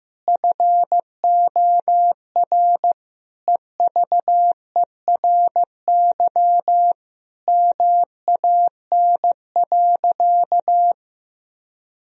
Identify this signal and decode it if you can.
Morse code, heard as a keyed tone, about 16 wpm, 700 Hz. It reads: FOR EVERY MAN.
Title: Sentences from Top 1000 Words - Review Part 1 - 15wpm